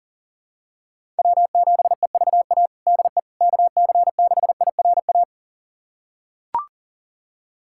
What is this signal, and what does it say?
Morse code, heard as a keyed tone, about 39 wpm, 700 Hz. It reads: W7EVA DE KC6IRA
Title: USA Callsign DE Callsign 40wpm